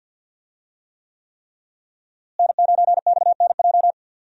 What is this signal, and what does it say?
Morse code, heard as a keyed tone, about 47 wpm, 700 Hz. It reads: N9XNJ
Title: USA Call Signs - 5 Characters - Repeated 3x - 50wpm